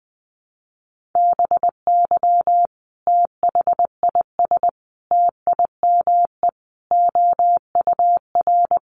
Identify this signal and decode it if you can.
Morse code, heard as a keyed tone, about 20 wpm, 700 Hz. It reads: BY THIS TIME OUR
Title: Sentences from Top 1400 Words 20wpm